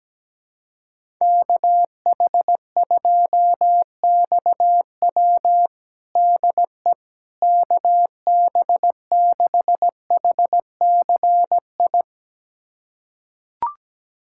Morse code, heard as a keyed tone, about 17 wpm, 700 Hz. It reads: KH2XW DE KB6HCI
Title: USA Callsign DE Callsign 17wpm